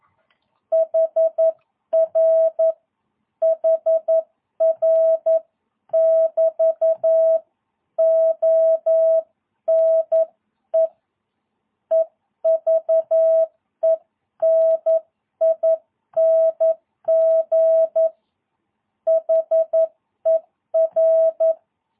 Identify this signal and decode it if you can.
Morse code, heard as a keyed tone, about 11 wpm, 660 Hz. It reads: HR HR <BT> ONE EVENING HER
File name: jh8cbh / 2022cw